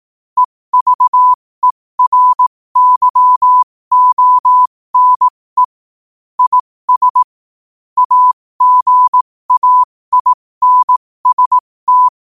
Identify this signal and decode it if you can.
Morse code, heard as a keyed tone, about 18 wpm, spaced as for 15 wpm, 1000 Hz. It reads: EVERYONE IS AGAINST
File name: qotdfiles / 15wpm